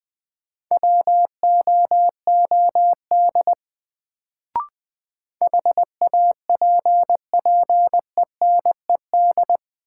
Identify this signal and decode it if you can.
Morse code, heard as a keyed tone, about 20 wpm, 700 Hz. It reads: WOOD HAPPENED
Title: Sentences from Top 1000 Words - Review Part 2 - New Word Drill - 20wpm